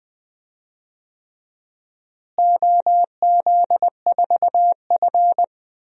Morse code, heard as a keyed tone, about 20 wpm, 700 Hz. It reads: OZ4F
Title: DXCC-100 Call Signs - 4 Characters - Repeated 3x - 20wpm